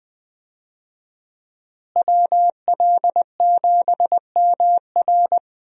Morse code, heard as a keyed tone, about 21 wpm, 700 Hz. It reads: WL7MR
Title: USA Call Signs - 5 Characters - Repeated 3x - 20wpm